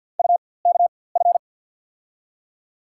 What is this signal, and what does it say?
Morse code, heard as a keyed tone, about 47 wpm, spaced as for 23 wpm, 700 Hz. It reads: UKF
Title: USA Call Signs - 6 Characters - 4x Character Spacing - 50wpm